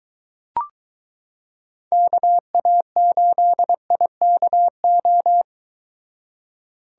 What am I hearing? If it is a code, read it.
Morse code, heard as a keyed tone, about 23 wpm, 700 Hz. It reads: KA8IKO